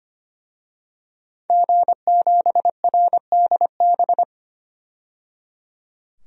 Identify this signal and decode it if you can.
Morse code, heard as a keyed tone, about 25 wpm, 700 Hz. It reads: G7RDB
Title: DXCC-100 Call Signs - 5 Characters - 25wpm